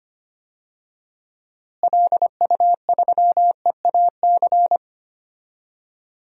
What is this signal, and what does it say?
Morse code, heard as a keyed tone, about 25 wpm, 700 Hz. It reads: LU3EAC
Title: DXCC-100 Call Signs - 6 Characters - Repeated 2x - 25wpm